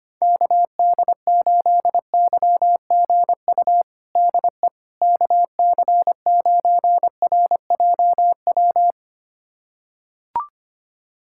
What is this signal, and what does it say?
Morse code, heard as a keyed tone, about 25 wpm, 700 Hz. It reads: KD8YGU DE KC9RJW
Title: USA Callsign DE Callsign - 6 Characters - 25wpm